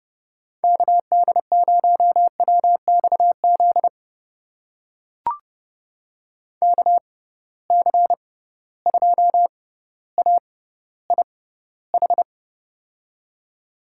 Morse code, heard as a keyed tone, about 31 wpm, spaced as for 15 wpm, 700 Hz. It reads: KD0WXZ KC2AIH
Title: USA Call Signs - 6 Characters - 6x Character Spacing - 30wpm